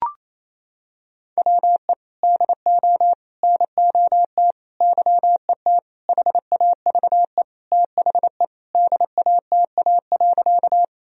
Morse code, heard as a keyed tone, about 28 wpm, 700 Hz. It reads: WE DO NOT YET HAVE THE DATA.
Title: Sentences from Top 1000 Words - Review Part 3 - 28wpm